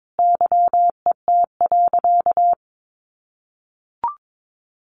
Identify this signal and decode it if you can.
Morse code, heard as a keyed tone, about 22 wpm, 700 Hz. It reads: YET.
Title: Easy Phrases 22wpm